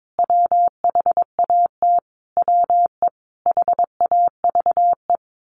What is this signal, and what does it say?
Morse code, heard as a keyed tone, about 22 wpm, 700 Hz. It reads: WHAT WE HAVE
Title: Sentences from Top 1500 Words 22wpm